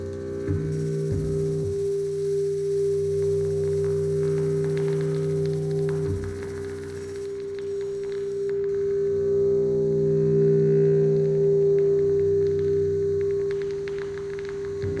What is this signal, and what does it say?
Dial tone (Japan), 400 Hz.